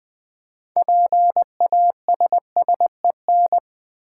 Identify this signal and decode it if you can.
Morse code, heard as a keyed tone, about 20 wpm, 700 Hz. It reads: PASSEN
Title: Build A Word - 3 Letters At A Time - 9-Letter Words - 20wpm